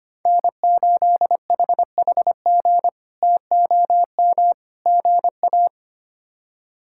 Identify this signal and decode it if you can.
Morse code, heard as a keyed tone, about 25 wpm, 700 Hz. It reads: N8HHG TOM GA
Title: K1USN SST QSO Practice - 25wpm